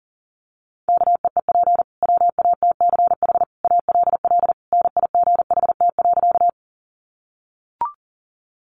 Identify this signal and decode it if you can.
Morse code, heard as a keyed tone, about 40 wpm, 700 Hz. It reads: KEEP WATCH ALL NIGHT.
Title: Sentences from Top 300 Words 40wpm